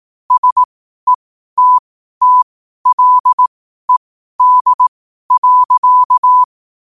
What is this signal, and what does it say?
Morse code, heard as a keyed tone, about 18 wpm, spaced as for 12 wpm, 1000 Hz. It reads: SETTLED.
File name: qotdfiles / 12wpm